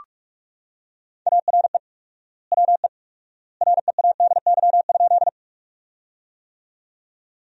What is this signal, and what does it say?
Morse code, heard as a keyed tone, about 45 wpm, 700 Hz. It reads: ARE WE READY?
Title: 4x Word-Spacing - Sentences from Top 400 Words - 45wpm